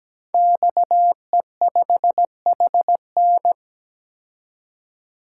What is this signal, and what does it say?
Morse code, heard as a keyed tone, about 17 wpm, 700 Hz. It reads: XE5HN
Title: Code Groups - 5 Letter:Numbers - 17wpm